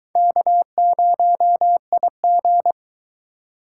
Morse code, heard as a keyed tone, about 22 wpm, 700 Hz. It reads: K0IG